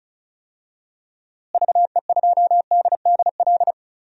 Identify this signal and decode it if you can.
Morse code, heard as a keyed tone, about 34 wpm, 700 Hz. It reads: VE2DDL
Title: DXCC-100 Call Signs - 6 Characters - Repeated 2x - 35wpm